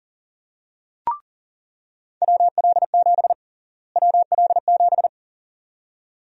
Morse code, heard as a keyed tone, about 40 wpm, 700 Hz. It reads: WL7 WL7
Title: USA Call Sign Prefixes - Repeated 2x - 40wpm